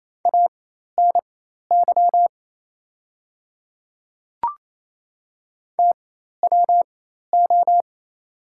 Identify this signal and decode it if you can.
Morse code, heard as a keyed tone, about 28 wpm, spaced as for 13 wpm, 690 Hz. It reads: ANY TWO
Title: 4x Character Spacing - Top 500 Words - 28wpm